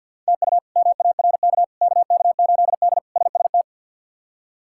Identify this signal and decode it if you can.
Morse code, heard as a keyed tone, about 47 wpm, 700 Hz. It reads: TU MARK KK8D SST
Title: K1USN SST QSO Practice - 50wpm